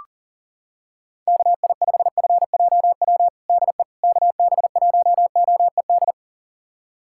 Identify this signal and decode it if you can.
Morse code, heard as a keyed tone, about 40 wpm, 700 Hz. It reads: KI5FJW DE KB1OED
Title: USA Callsign DE Callsign - 6 Characters - 40wpm